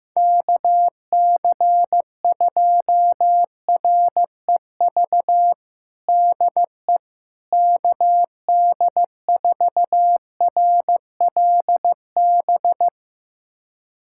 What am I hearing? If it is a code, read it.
Morse code, heard as a keyed tone, about 15 wpm, 700 Hz. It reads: KC2REV DE KD4RLB
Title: USA Callsign DE Callsign - 6 Characters - 15wpm